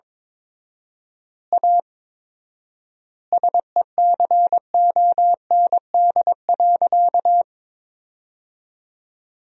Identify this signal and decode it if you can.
Morse code, heard as a keyed tone, about 22 wpm, 700 Hz. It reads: A SECOND.
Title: 4x Word-Spacing - Sentences from Top 500 Words - 22wpm